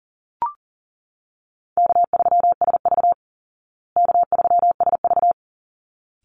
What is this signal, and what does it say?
Morse code, heard as a keyed tone, about 40 wpm, 700 Hz. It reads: K3SV K3SV
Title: Easy Call Signs Repeated 2x - 40wpm